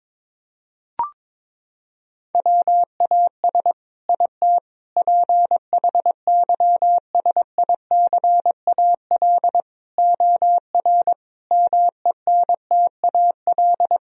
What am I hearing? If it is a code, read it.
Morse code, heard as a keyed tone, about 22 wpm, 700 Hz. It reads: WAS IT PHYSICAL OR MENTAL
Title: Sentences from Top 1000 Words - Review Part 4 - 22wpm